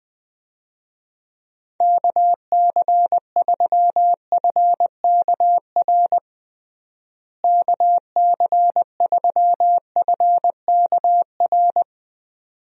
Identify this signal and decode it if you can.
Morse code, heard as a keyed tone, about 21 wpm, 700 Hz. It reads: KC3FKR KC3FKR
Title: USA Call Signs - 6 Characters - Repeated 2x - 20wpm